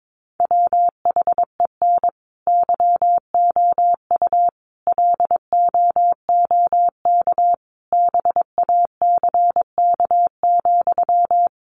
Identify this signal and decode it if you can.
Morse code, heard as a keyed tone, about 22 wpm, 700 Hz. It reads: WHEN YOU LOOK BACK,
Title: Sentences from Top 800 Words 22wpm